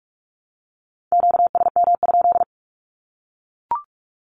Morse code, heard as a keyed tone, about 44 wpm, 700 Hz. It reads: QSM?
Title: Q-Codes 45wpm